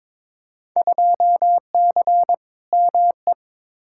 Morse code, heard as a keyed tone, about 22 wpm, 700 Hz. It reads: 2C ME
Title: Field Day Exchanges Encore - 22wpm